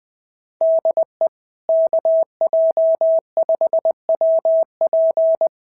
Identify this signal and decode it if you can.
Morse code, heard as a keyed tone, about 19 wpm, 650 Hz. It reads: DE KJ5WP
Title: SOTA QSO Practice 20wpm